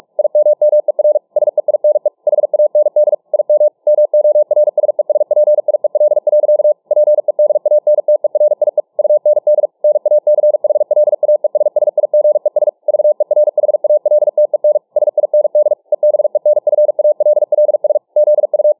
Morse code, heard as a keyed tone, about 44 wpm, 600 Hz. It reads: IMMER SEINE HAND IM MORSESPIEL. PEDANTERIE UND NACHLAESSIGES VERHALTEN SIND EBENFALLS ZU